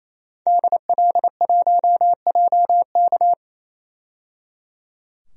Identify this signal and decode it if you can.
Morse code, heard as a keyed tone, about 28 wpm, 700 Hz. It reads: DL1JK
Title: DXCC-100 Call Signs - 5 Characters - 28wpm